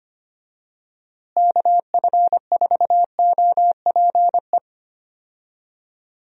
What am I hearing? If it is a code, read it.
Morse code, heard as a keyed tone, about 25 wpm, 700 Hz. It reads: KF4OPE
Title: USA Call Signs - 6 Characters - Repeated 2x - 25wpm